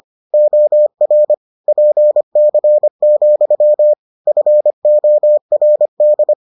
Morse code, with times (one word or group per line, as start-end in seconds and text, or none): 0.3-1.3 OR
1.7-3.9 PC,
4.3-6.3 FORD